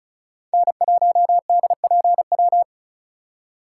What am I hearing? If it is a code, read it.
Morse code, heard as a keyed tone, about 34 wpm, 700 Hz. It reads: N1DPW
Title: USA Call Signs - 4x Character Spacing - 35wpm